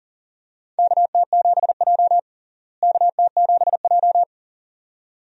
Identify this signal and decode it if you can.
Morse code, heard as a keyed tone, about 40 wpm, 700 Hz. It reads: KT7J KT7J
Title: USA Call Signs - 4 Characters - Repeated 2x - 40wpm